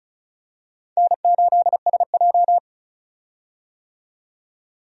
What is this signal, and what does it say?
Morse code, heard as a keyed tone, about 34 wpm, 700 Hz. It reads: N8SJ
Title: USA Call Signs - 4 Characters - 35wpm